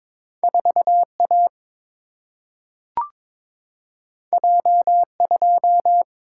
Morse code, heard as a keyed tone, about 22 wpm, 700 Hz. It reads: 4 A J 2
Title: Single-Letter Single-Number 22wpm